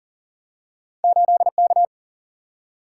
Morse code, heard as a keyed tone, about 42 wpm, 700 Hz. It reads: 8K
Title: Single-Letter Single-Number 40wpm